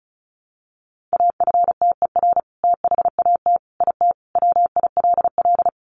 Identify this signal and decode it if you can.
Morse code, heard as a keyed tone, about 34 wpm, 700 Hz. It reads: AFTER THAT IT WILL
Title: Sentences from Top 1300 Words 35wpm